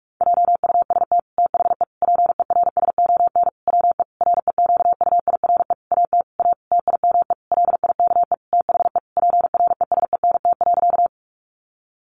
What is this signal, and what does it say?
Morse code, heard as a keyed tone, about 44 wpm, 700 Hz. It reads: JUST THE PERSON WE REQUIRE AT A TIME LIKE THE PRESENT.
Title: Sentences from Top 900 Words 45wpm